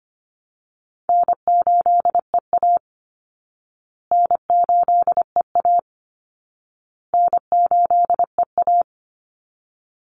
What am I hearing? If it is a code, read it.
Morse code, heard as a keyed tone, about 25 wpm, 700 Hz. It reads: N8EA N8EA N8EA
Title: Call Signs Repeated 2x - 25wpm